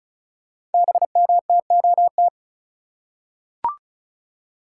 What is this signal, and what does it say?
Morse code, heard as a keyed tone, about 36 wpm, 700 Hz. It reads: BMTOT
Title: Code Groups - 5 Letters - 35wpm